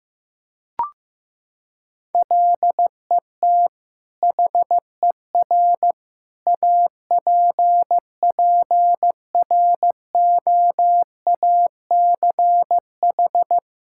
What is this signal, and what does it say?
Morse code, heard as a keyed tone, about 15 wpm, 700 Hz. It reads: LET HER APPROACH
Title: Sentences from Top 1000 Words - Review Part 6 - 15wpm